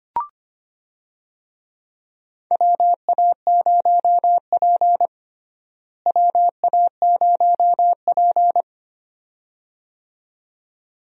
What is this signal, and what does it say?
Morse code, heard as a keyed tone, about 25 wpm, 700 Hz. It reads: WA0P WA0P
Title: USA Call Signs - 4 Characters - Repeated 2x - 25wpm